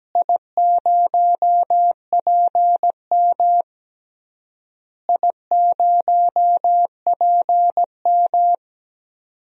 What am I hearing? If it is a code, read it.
Morse code, heard as a keyed tone, about 17 wpm, 700 Hz. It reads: I0PM I0PM
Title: DXCC-100 Call Signs - 4 Characters - Repeated 3x - 17wpm